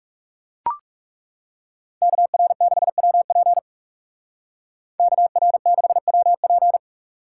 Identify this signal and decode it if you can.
Morse code, heard as a keyed tone, about 35 wpm, 700 Hz. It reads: KR6WP KR6WP
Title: Speed Racing - USA 2x2 Call Signs 30wpm